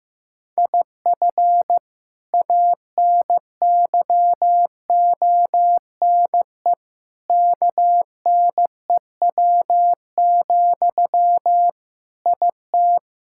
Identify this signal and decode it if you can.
Morse code, heard as a keyed tone, about 15 wpm, 700 Hz. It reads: IF ANYONE KNEW, IT
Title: Sentences from Top 1000 Words - Review Part 6 - 15wpm